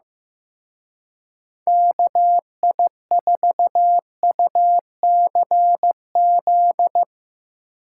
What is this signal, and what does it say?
Morse code, heard as a keyed tone, about 16 wpm, 700 Hz. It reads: KI4UCZ
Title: Speed Racing - USA 2x3 Call Signs 15wpm